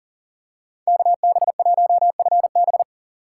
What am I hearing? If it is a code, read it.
Morse code, heard as a keyed tone, about 39 wpm, 700 Hz. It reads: KB1FB
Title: USA Call Signs - 5 Characters - 6x Character Spacing - 40wpm